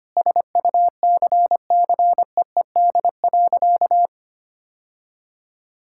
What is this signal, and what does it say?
Morse code, heard as a keyed tone, about 25 wpm, 700 Hz. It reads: SUCCEED.
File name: practice / 25wpm / Sentences with 2-Word Phrases - 25wpm